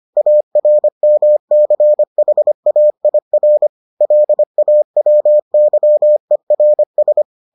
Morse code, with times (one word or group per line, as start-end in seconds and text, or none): 0.2-3.7 ARMCHAIR
4.0-7.2 LAWYERS